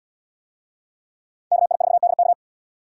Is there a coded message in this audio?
Yes: Morse code from 1.5 to 2.3 s, heard as a keyed tone, about 66 wpm, 700 Hz. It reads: KI4DK